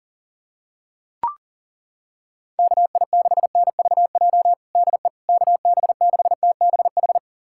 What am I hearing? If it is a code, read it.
Morse code, heard as a keyed tone, about 37 wpm, 700 Hz. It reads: KI6NVJ DE KB6TBH
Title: USA Callsign DE Callsign - 6 Characters - 40wpm